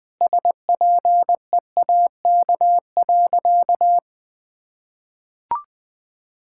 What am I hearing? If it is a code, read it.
Morse code, heard as a keyed tone, about 20 wpm, 700 Hz. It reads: SPEAK.
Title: Sentences from Top 1600 Words 20wpm